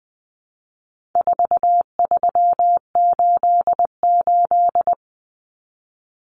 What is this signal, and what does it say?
Morse code, heard as a keyed tone, about 20 wpm, 700 Hz. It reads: 4388
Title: Common Cut Numbers - 4-Digit 20wpm